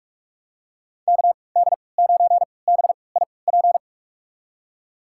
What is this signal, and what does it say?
Morse code, heard as a keyed tone, about 47 wpm, spaced as for 26 wpm, 700 Hz. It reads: KD9BIP
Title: USA Call Signs - 6 Characters - 3x Character Spacing - 45wpm